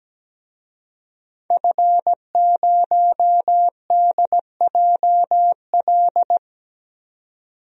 Morse code, heard as a keyed tone, about 17 wpm, 700 Hz. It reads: F0DJL
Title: DXCC-100 Call Signs - 5 Characters - 17wpm